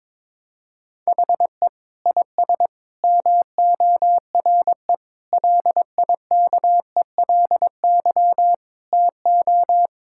Morse code, heard as a keyed tone, about 22 wpm, 700 Hz. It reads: HE IS MORE LIKELY TO
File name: practice / 22wpm / Sentences with 4-Word Phrases - 22wpm